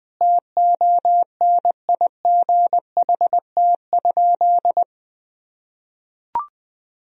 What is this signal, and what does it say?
Morse code, heard as a keyed tone, about 20 wpm, 700 Hz. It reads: TONIGHT?
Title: Sentences from Top 1000 Words - Review Part 2 - 20wpm